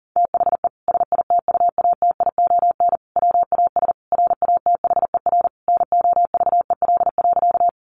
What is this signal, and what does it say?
Morse code, heard as a keyed tone, about 40 wpm, 700 Hz. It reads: THE SITUATION WAS RATHER NOVEL.
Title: Sentences from Top 1300 Words 40wpm